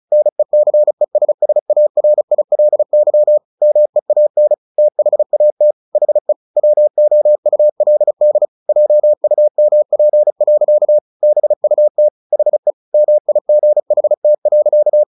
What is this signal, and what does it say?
Morse code, heard as a keyed tone, about 34 wpm, 600 Hz. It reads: NECESSARILY MEAN THAT HE WOULD JUMP. BUT HE MIGHT.
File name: the old man and the sea / the old man and the sea 1 35wpm